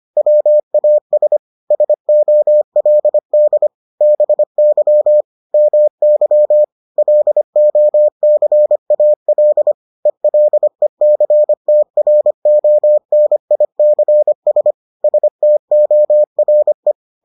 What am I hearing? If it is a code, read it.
Morse code, heard as a keyed tone, about 25 wpm, 600 Hz. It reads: WAS SOLD BY MY LOCAL ELECTRONICS STORE